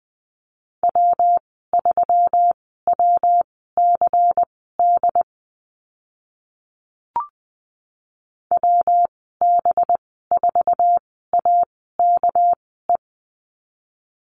Morse code, heard as a keyed tone, about 20 wpm, spaced as for 14 wpm, 700 Hz. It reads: W3WCD WB4AKE
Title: 219 Lesson 35g - Number 0 - Call Signs 20wpm